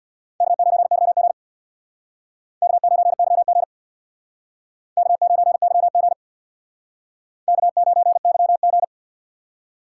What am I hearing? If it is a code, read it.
Morse code, heard as a keyed tone, about 61 wpm, 700 Hz. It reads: K9YG K9YG K9YG K9YG
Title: Speed Racing - USA 1x2 Call Signs 50wpm